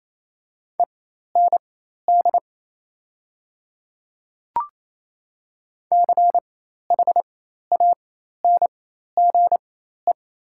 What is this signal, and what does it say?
Morse code, heard as a keyed tone, about 28 wpm, spaced as for 13 wpm, 690 Hz. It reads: END CHANGE
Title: 4x Character Spacing - Top 200 Words - 28wpm